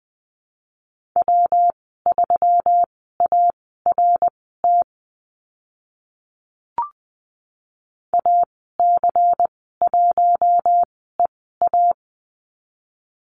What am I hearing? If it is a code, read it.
Morse code, heard as a keyed tone, about 20 wpm, spaced as for 15 wpm, 700 Hz. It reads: W3ART AC1EA
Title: 154 Lesson 25g - Letter p - Call Signs 20wpm